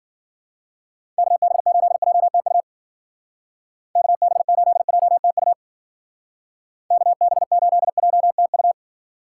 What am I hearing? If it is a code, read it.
Morse code, heard as a keyed tone, about 50 wpm, 700 Hz. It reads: KB8JTU KB8JTU KB8JTU
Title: Speed Racing - USA 2x3 Call Signs 40wpm